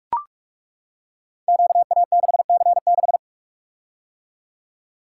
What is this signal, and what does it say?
Morse code, heard as a keyed tone, about 43 wpm, 700 Hz. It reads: QA6C6
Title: Code Groups - 5 Letter:Numbers - 45wpm